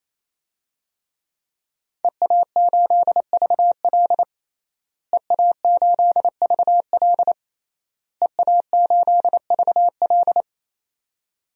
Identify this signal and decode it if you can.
Morse code, heard as a keyed tone, about 28 wpm, 700 Hz. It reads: EA8VL EA8VL EA8VL
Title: DXCC-100 Call Signs - 5 Characters - Repeated 3x - 28wpm